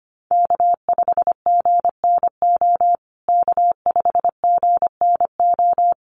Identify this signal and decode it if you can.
Morse code, heard as a keyed tone, about 25 wpm, 700 Hz. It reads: K5GNO K5GNO
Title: USA Call Signs Repeated 2x 25wpm